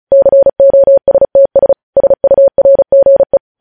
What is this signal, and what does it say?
Morse code, heard as a keyed tone, about 36 wpm, 550 Hz. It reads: COSTS SURGE
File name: podcast / News headlines-1x-35x35